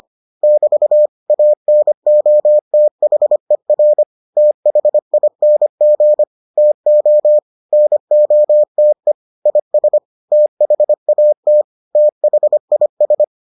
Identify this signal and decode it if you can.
Morse code, heard as a keyed tone, about 25 wpm, 600 Hz. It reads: <BT> ANOTHER THING TO NOTE IS THAT THIS